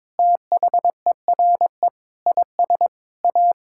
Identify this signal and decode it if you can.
Morse code, heard as a keyed tone, about 22 wpm, 700 Hz. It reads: THERE IS A
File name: practice / 22wpm / Sentences with 4-Word Phrases - 22wpm